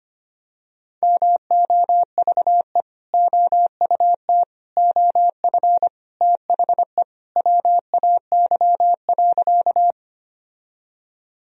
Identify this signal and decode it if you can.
Morse code, heard as a keyed tone, about 25 wpm, 700 Hz. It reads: MOVE OUT OF THE WAY.
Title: Sentences from Top 1000 Words - Review Part 1 - 25wpm